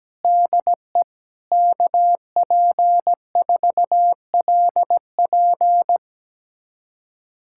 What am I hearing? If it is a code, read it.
Morse code, heard as a keyed tone, about 17 wpm, 700 Hz. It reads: DE KP4LP
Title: USA Callsign DE Callsign 17wpm